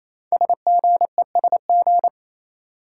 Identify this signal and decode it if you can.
Morse code, heard as a keyed tone, about 29 wpm, 700 Hz. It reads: SGESG
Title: Code Groups - 5 Letters - 28wpm